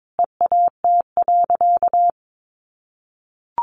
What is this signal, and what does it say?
Morse code, heard as a keyed tone, about 22 wpm, 700 Hz. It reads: EAT.
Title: Sentences from Top 600 Words 22wpm